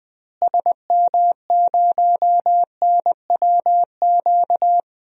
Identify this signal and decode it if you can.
Morse code, heard as a keyed tone, about 20 wpm, 700 Hz. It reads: SM0NWQ
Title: DXCC-100 Call Signs - 6 Characters - Repeated 3x - 20wpm